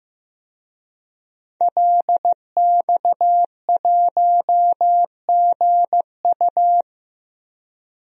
Morse code, heard as a keyed tone, about 15 wpm, 700 Hz. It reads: LX1GU
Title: DXCC-100 Call Signs - 5 Characters - 15wpm